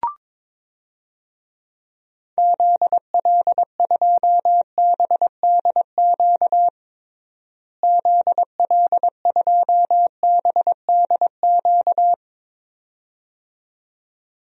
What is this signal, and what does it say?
Morse code, heard as a keyed tone, about 22 wpm, 700 Hz. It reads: ZL2BDQ ZL2BDQ
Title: DXCC-100 Call Signs - 6 Characters - Repeated 2x - 22wpm